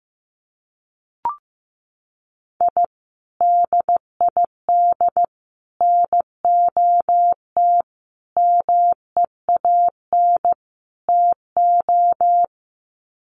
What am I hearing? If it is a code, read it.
Morse code, heard as a keyed tone, about 15 wpm, 700 Hz. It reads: I DID NOT MEAN TO
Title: Sets of 5 Words 15wpm